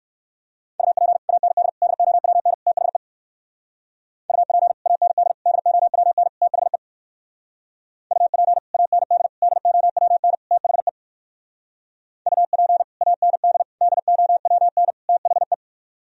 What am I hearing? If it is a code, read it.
Morse code, heard as a keyed tone, about 58 wpm, 700 Hz. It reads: UP AND DOWN THE UP AND DOWN THE UP AND DOWN THE UP AND DOWN THE